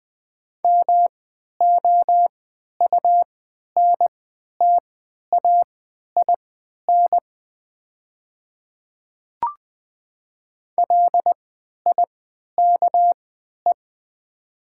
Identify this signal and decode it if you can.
Morse code, heard as a keyed tone, about 20 wpm, spaced as for 11 wpm, 700 Hz. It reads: MOUNTAIN LIKE
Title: 3x Character Spacing - Top 300 Words - 20wpm